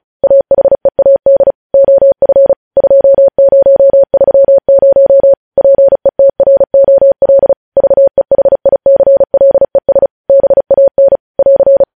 Morse code, heard as a keyed tone, about 36 wpm, 550 Hz. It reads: AHEAD OF 2030 PETROL VEHICLES BAN <AR>
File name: podcast / News headlines-1x-35x35